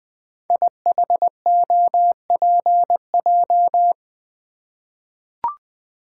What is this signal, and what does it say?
Morse code, heard as a keyed tone, about 20 wpm, 700 Hz. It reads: IHOPJ
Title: Code Groups - 5 Letters - 20wpm